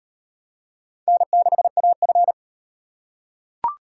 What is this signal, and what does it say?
Morse code, heard as a keyed tone, about 36 wpm, 700 Hz. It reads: N6AF